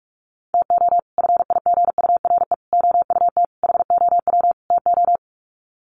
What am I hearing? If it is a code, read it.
Morse code, heard as a keyed tone, about 44 wpm, 700 Hz. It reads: TO FIGURE OUT HOW TO
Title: Sets of 5 Words 45wpm